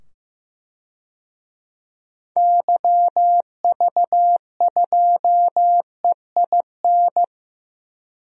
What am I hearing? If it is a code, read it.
Morse code, heard as a keyed tone, about 15 wpm, 700 Hz. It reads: YV2EIN
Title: DXCC-100 Call Signs - 6 Characters - 15wpm